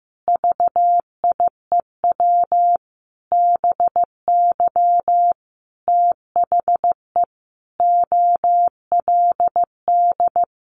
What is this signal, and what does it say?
Morse code, heard as a keyed tone, about 15 wpm, 700 Hz. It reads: VIEW BY THE OLD
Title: Sentences from Top 1200 Words 15wpm